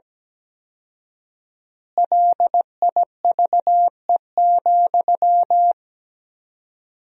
Morse code, heard as a keyed tone, about 17 wpm, 700 Hz. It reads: LIVE,
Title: 4x Word-Spacing - Sentences from Top 200 Words - 17wpm